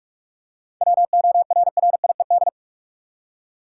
Morse code, heard as a keyed tone, about 42 wpm, 700 Hz. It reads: WORRIED
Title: Seven Letter Words - 45wpm